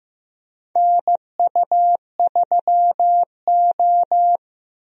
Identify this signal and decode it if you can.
Morse code, heard as a keyed tone, about 15 wpm, 700 Hz. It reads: NU3O
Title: USA Call Signs - 4 Characters - Repeated 3x - 15wpm